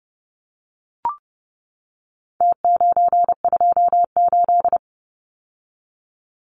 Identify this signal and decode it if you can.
Morse code, heard as a keyed tone, about 30 wpm, 700 Hz. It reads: T928
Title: Numbers - Contest Style - 4-Digit 30wpm